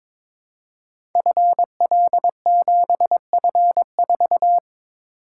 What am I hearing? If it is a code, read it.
Morse code, heard as a keyed tone, about 23 wpm, 700 Hz. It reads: FL7F4
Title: Code Groups - 5 Letter:Numbers - 22wpm